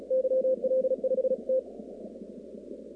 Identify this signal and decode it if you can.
Morse code, heard as a keyed tone, about 36 wpm, 500 Hz. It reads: YL5T
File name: contests / 2019-cw-wpx-kt5j-k5pi- -2330 0-38